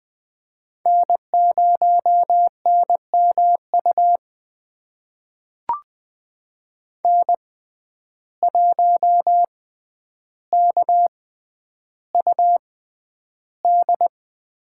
Morse code, heard as a keyed tone, about 20 wpm, spaced as for 10 wpm, 700 Hz. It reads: N0NMU N1KUD
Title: USA Call Signs - 5 Characters - 6x Character Spacing - 20wpm